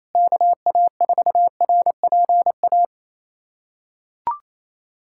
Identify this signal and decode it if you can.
Morse code, heard as a keyed tone, about 27 wpm, 700 Hz. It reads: KA4RPA
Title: USA Callsign DE Callsign 28wpm